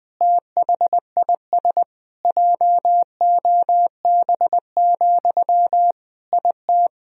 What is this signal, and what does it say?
Morse code, heard as a keyed tone, about 20 wpm, 700 Hz. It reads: THIS JOB, IT
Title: Sentences from Top 1000 Words - Review Part 4 - 20wpm